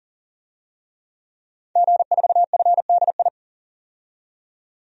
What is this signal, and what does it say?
Morse code, heard as a keyed tone, about 37 wpm, 700 Hz. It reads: G4FDI
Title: DXCC-100 Call Signs - 5 Characters - 40wpm